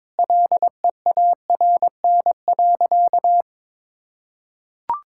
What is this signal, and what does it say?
Morse code, heard as a keyed tone, about 22 wpm, 700 Hz. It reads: LEARN.
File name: Sentences with 7-Word Phrases - 22wpm